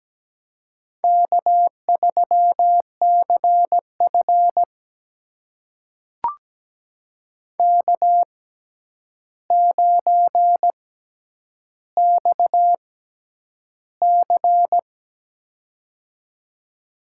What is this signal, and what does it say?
Morse code, heard as a keyed tone, about 17 wpm, spaced as for 9 wpm, 700 Hz. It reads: K3CF K9XC
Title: USA Call Signs - 4 Characters - 6x Character Spacing - 17wpm